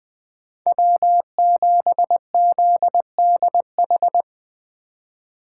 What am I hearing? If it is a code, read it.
Morse code, heard as a keyed tone, about 20 wpm, 700 Hz. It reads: W7ZDH